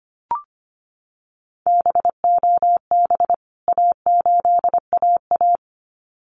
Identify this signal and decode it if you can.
Morse code, heard as a keyed tone, about 25 wpm, 700 Hz. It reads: BOB A8AA
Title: CWT Exchange 25wpm